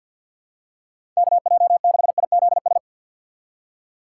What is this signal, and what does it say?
Morse code, heard as a keyed tone, about 48 wpm, 700 Hz. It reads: KJ6IZS
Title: USA Call Signs - 6 Characters - 3x Character Spacing - 50wpm